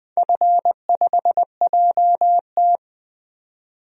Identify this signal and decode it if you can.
Morse code, heard as a keyed tone, about 20 wpm, 700 Hz. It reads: F5JT
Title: DXCC-100 Call Signs - 4 Characters - 20wpm